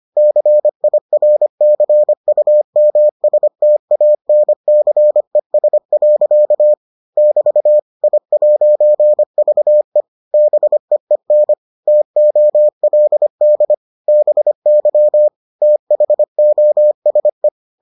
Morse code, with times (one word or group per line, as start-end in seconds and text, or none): 0.2-6.7 CIRCUMSTANCES.
7.2-7.8 <BT>
8.0-10.0 I'VE
10.3-11.5 BEEN
11.9-13.7 TOLD
14.1-15.3 BY
15.6-17.5 THOSE